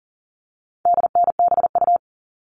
Morse code, heard as a keyed tone, about 40 wpm, 700 Hz. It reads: DNBU
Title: All Cut Numbers - 4-Digit 40wpm